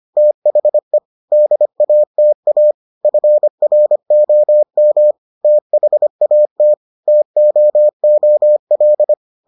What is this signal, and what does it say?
Morse code, heard as a keyed tone, about 25 wpm, 600 Hz. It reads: THE DATA FROM THAT TOOL